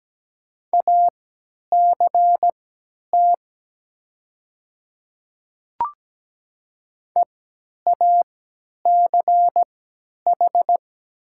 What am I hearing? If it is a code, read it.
Morse code, heard as a keyed tone, about 17 wpm, spaced as for 10 wpm, 700 Hz. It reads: ACT EACH
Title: 3x Character Spacing - Top 200 Words - 17wpm